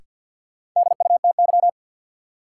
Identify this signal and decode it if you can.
Morse code, heard as a keyed tone, about 53 wpm, 700 Hz. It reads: DUTY
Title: Sentences from Top 1500 Words - New Word Drill Alt - 50wpm